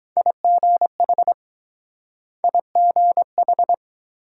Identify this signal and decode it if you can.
Morse code, heard as a keyed tone, about 23 wpm, 700 Hz. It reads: IGH IGH